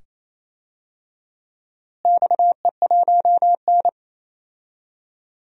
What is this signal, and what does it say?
Morse code, heard as a keyed tone, about 28 wpm, 700 Hz. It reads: XE1N
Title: DXCC-100 Call Signs - 4 Characters - Repeated 2x - 28wpm